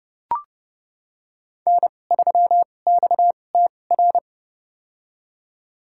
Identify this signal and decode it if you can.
Morse code, heard as a keyed tone, about 31 wpm, spaced as for 22 wpm, 700 Hz. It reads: N3XTR
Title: USA Call Signs - 5 Characters - 2x Character Spacing - 30wpm